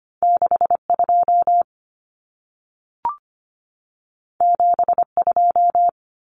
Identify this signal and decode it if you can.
Morse code, heard as a keyed tone, about 25 wpm, 700 Hz. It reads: 62 72
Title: Numbers - 2-Digit 25wpm